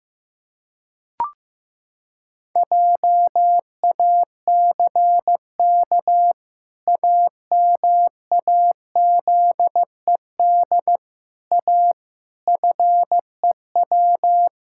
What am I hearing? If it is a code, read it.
Morse code, heard as a keyed tone, about 15 wpm, 700 Hz. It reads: JACK AMAZED A FEW
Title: Pangram Challenge v2 - 15wpm